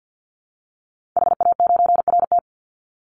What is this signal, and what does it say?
Morse code, heard as a keyed tone, about 50 wpm, 700 Hz. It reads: HA9RT
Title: Call Signs Repeated 2x - 50wpm